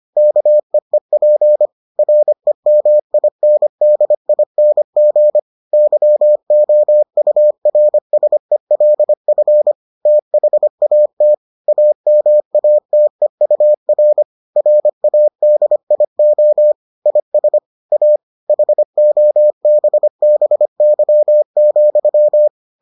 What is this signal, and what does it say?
Morse code, heard as a keyed tone, about 25 wpm, 600 Hz. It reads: KEEP REMINDING YOURSELF THAT AMATEUR RADIO IS A HOBBY,